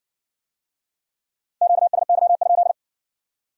Morse code, heard as a keyed tone, about 60 wpm, 700 Hz. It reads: QSY?